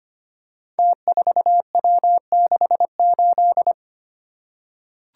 Morse code, heard as a keyed tone, about 25 wpm, 700 Hz. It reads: T4W68
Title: Code Groups - 5 Letter:Numbers - 25wpm